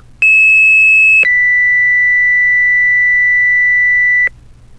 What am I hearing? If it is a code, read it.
Two-tone page, heard as a one-second tone, then a three-second tone, about 2600 then 2000 Hz, with noise. It is heard from 0.2 to 4.3 s.
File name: Fire-Pager